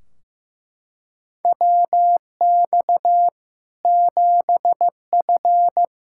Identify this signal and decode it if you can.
Morse code, heard as a keyed tone, about 16 wpm, 700 Hz. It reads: WX 7F
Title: K1USN SST QSO Practice 15wpm